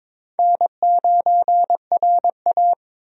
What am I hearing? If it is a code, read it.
Morse code, heard as a keyed tone, about 22 wpm, 700 Hz. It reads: N9RA
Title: USA Call Signs - 4 Characters - 22wpm